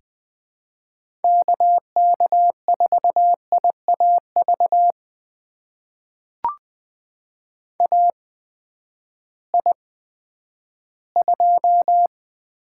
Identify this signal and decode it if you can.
Morse code, heard as a keyed tone, about 20 wpm, spaced as for 11 wpm, 700 Hz. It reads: KK4IAV AI2
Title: USA Call Signs - 8x Character Spacing - 20wpm